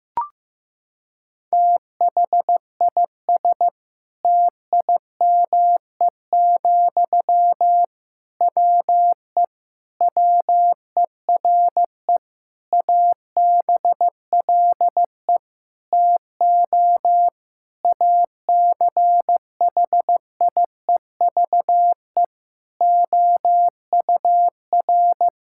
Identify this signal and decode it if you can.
Morse code, heard as a keyed tone, about 15 wpm, 700 Hz. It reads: THIS TIME, WE WERE ABLE TO ACHIEVE OUR
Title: Sentences from Top 1000 Words - Review Part 6 - 15wpm